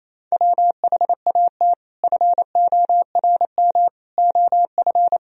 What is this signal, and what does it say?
Morse code, heard as a keyed tone, about 28 wpm, 700 Hz. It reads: WHAT FORM OF
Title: Sentences from Top 1000 Words - Review Part 3 - 28wpm